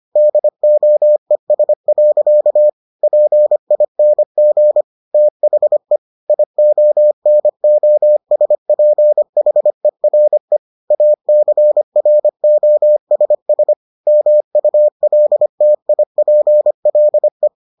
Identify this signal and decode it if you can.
Morse code, heard as a keyed tone, about 25 wpm, 600 Hz. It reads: DOES. PING THE IONOSPHERE ACROSS MULTIPLE